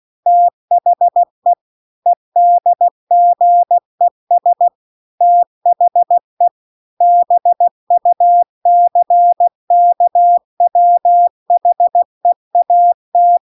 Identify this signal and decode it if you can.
Morse code, heard as a keyed tone, about 16 wpm, 700 Hz. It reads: THE EDGES THE BUCKWHEAT
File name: CW-The-Big-Twoheaded-River-2-16WPM